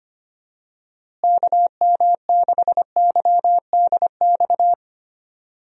Morse code, heard as a keyed tone, about 25 wpm, 700 Hz. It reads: KM6YDX
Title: USA Call Signs - 6 Characters - Repeated 3x - 25wpm